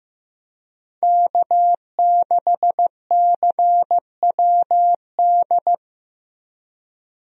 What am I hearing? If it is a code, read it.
Morse code, heard as a keyed tone, about 15 wpm, 700 Hz. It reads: K6CWD
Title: USA Call Signs - 5 Characters - Repeated 2x - 15wpm